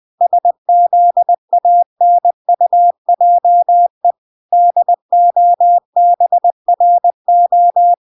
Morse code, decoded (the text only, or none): SZANUJE DOBRO